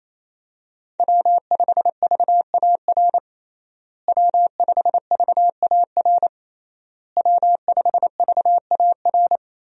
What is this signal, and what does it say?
Morse code, heard as a keyed tone, about 28 wpm, 700 Hz. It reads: W5VAR W5VAR W5VAR
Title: USA Call Signs - 5 Characters - Repeated 3x - 28wpm